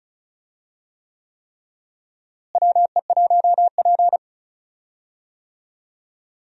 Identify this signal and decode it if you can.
Morse code, heard as a keyed tone, about 36 wpm, 700 Hz. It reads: WE1P
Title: USA Call Signs - 4 Characters - Repeated 3x - 35wpm